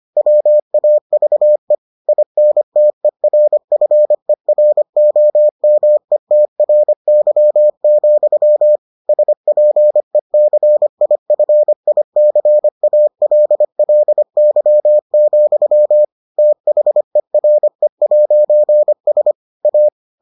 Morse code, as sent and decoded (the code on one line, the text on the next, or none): .-- .- ...- . / .. -. - . .-. ..-. .-. --- -- . - .-. -.-- --..-- / ... .--. -.-. .. ..-. .. -.-. .- .-.. .-.. -.-- --..-- / - .... .-. .----. ... / .-
WAVE INTERFEROMETRY, SPECIFICALLY, THERE'S A